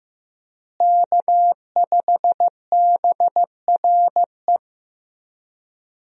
Morse code, heard as a keyed tone, about 15 wpm, 700 Hz. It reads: K5BRE
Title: USA Call Signs - 5 Characters - Repeated 3x - 15wpm